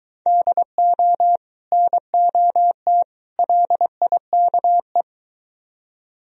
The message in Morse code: DO NOT LIKE

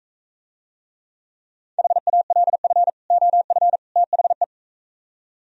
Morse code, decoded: HALF OF THE

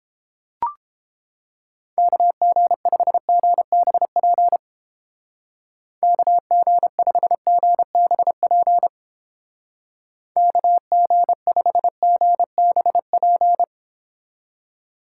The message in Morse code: KG5GBP KG5GBP KG5GBP

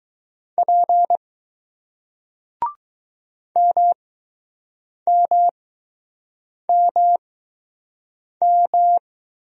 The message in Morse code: P M M M M

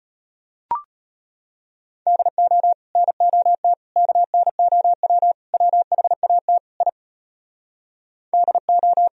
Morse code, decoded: DO NOT KNOW WHAT I DO

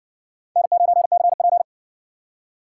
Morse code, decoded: N9CP

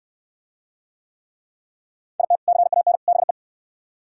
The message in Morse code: IT CAN BE

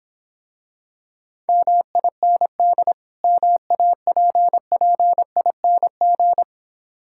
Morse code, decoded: MIND MAPPING

